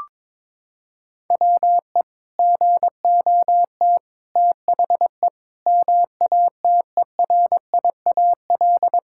Morse code, heard as a keyed tone, about 22 wpm, 700 Hz. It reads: WE GOT THE MATERIAL
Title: Sentences from Top 1000 Words - Review Part 5 - 22wpm